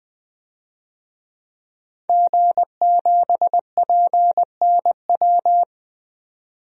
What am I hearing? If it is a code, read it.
Morse code, heard as a keyed tone, about 21 wpm, 700 Hz. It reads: G7PNW
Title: DXCC-100 Call Signs - 5 Characters - 20wpm